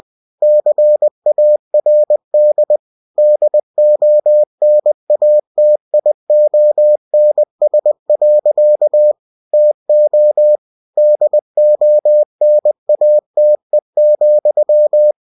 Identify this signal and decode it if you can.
Morse code, heard as a keyed tone, about 20 wpm, 600 Hz. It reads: CARD DONATIONS. TO DONATE,
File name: cw-books / winnie-the-pooh / winnie-20wpm-page-0122